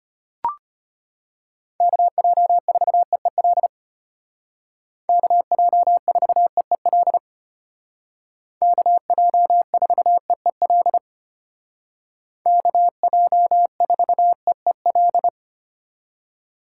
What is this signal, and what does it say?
Morse code, heard as a keyed tone, about 32 wpm, 700 Hz. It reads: KJ4EEL KJ4EEL KJ4EEL KJ4EEL